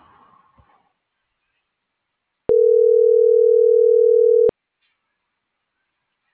Ringback tone (North America), 440 + 480 Hz.